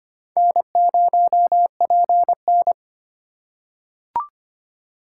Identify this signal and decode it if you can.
Morse code, heard as a keyed tone, about 25 wpm, 700 Hz. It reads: N0PN
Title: USA Call Signs - 4 Characters - 4x Character Spacing - 25wpm